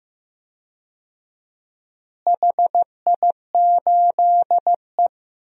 Morse code, heard as a keyed tone, about 16 wpm, 700 Hz. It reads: HI8E
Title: DXCC-100 Call Signs - 4 Characters - Repeated 3x - 15wpm